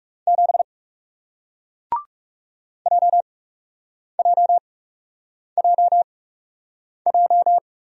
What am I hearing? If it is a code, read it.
Morse code, heard as a keyed tone, about 40 wpm, spaced as for 8 wpm, 700 Hz. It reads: 7 JJJJ